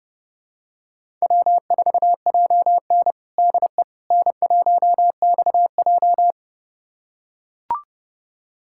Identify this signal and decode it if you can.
Morse code, heard as a keyed tone, about 30 wpm, 700 Hz. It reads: W4JN DE N1XJ
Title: USA Callsign DE Callsign - 4 Characters - 30wpm